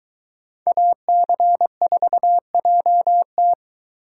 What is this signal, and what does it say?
Morse code, heard as a keyed tone, about 23 wpm, 700 Hz. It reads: AC4JT